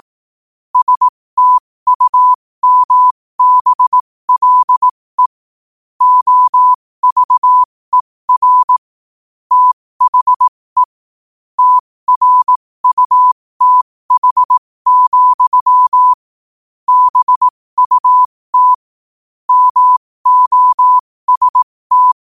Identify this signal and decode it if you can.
Morse code, heard as a keyed tone, about 18 wpm, spaced as for 15 wpm, 1000 Hz. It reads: STUMBLE OVER THE TRUTH, BUT MOST
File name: qotdfiles / 15wpm